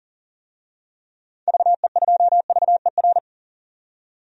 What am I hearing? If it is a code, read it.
Morse code, heard as a keyed tone, about 40 wpm, 700 Hz. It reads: VE2VER
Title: DXCC-100 Call Signs - 6 Characters - 40wpm